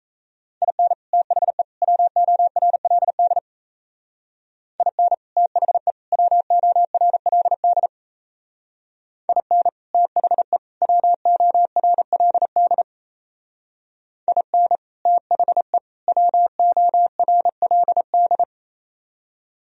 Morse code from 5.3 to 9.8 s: THE WORLD IN

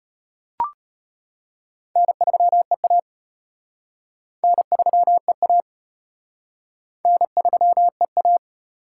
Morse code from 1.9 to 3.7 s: N3EA

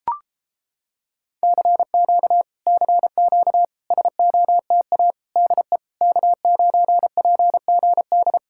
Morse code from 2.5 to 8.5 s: CQ SOTA DE K9PGD

